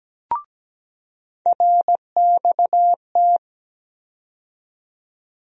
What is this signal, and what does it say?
Morse code, heard as a keyed tone, about 17 wpm, 700 Hz. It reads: RXT
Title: USA Call Sign Postfixes - 3 Letters 17wpm